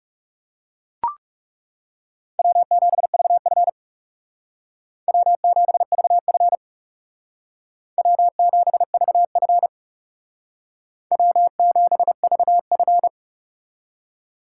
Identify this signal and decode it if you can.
Morse code, heard as a keyed tone, about 38 wpm, 700 Hz. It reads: W7VF W7VF W7VF W7VF